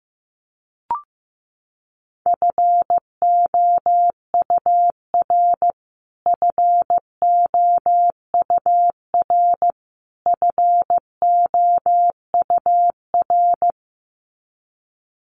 Morse code, heard as a keyed tone, about 15 wpm, 700 Hz. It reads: FOUR FOUR FOUR
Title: Top 500 Words Repeated 15wpm